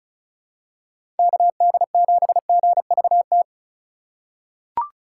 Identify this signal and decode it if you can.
Morse code, heard as a keyed tone, about 36 wpm, 700 Hz. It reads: KD7GVT
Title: USA Call Signs - 6 Characters - 2x Character Spacing - 35wpm